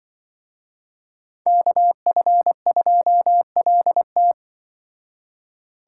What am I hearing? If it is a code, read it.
Morse code, heard as a keyed tone, about 23 wpm, 700 Hz. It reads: KF2LT
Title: Speed Racing - USA 2x2 Call Signs 20wpm